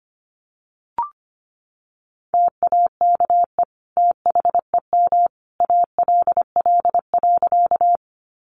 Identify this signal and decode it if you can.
Morse code, heard as a keyed tone, about 25 wpm, 700 Hz. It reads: TAKE THEM ALL.
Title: Sentences from Top 200 Words 25wpm